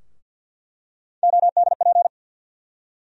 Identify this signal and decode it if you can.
Morse code, heard as a keyed tone, about 48 wpm, 700 Hz. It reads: ODP
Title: USA Call Sign Postfixes - 3 Letters 50wpm